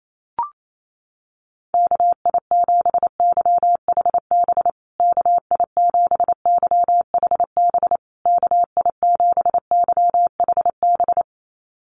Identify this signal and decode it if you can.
Morse code, heard as a keyed tone, about 28 wpm, 690 Hz. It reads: KI7YHB KI7YHB KI7YHB
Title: USA Call Signs Repeated 3x 28wpm